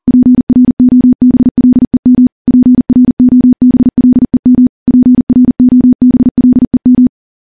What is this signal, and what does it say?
Morse code, heard as a keyed tone, about 40 wpm, 250 Hz. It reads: PROBLEM PROBLEM PROBLEM
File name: morse / morse-training-40WPM